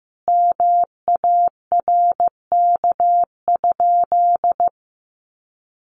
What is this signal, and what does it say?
Morse code, heard as a keyed tone, about 15 wpm, 700 Hz. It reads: MARK?
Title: Sentences from Top 300 Words 15wpm